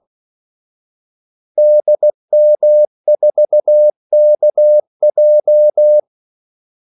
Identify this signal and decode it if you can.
Morse code, heard as a keyed tone, about 16 wpm, 600 Hz. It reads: DM4KJ
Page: Kaldesignaler 16 wpm